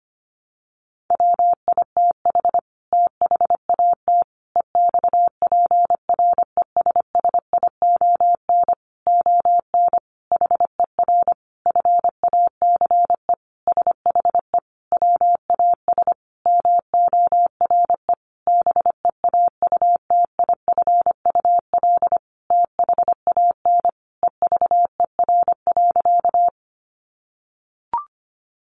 Morse code, heard as a keyed tone, about 25 wpm, 700 Hz. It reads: WITH THAT EXPRESSION ON HER FACE SHE WAS MORE BEAUTIFUL THAN EVER.
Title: Sentences from Top 1200 Words 25wpm